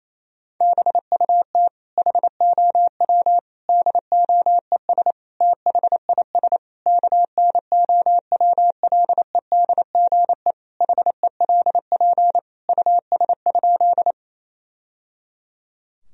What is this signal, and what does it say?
Morse code, heard as a keyed tone, about 27 wpm, 700 Hz. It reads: BUT HOW DOES THIS KNOWLEDGE HELP US?
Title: Sentences from Top 1000 Words - Review Part 3 - 28wpm